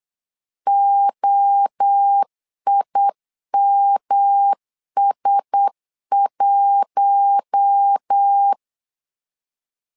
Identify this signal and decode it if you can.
Morse code, heard as a keyed tone, about 8 wpm, 800 Hz. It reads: OIMS1
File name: LEZIONI / 4 Q1S 8wpm 2710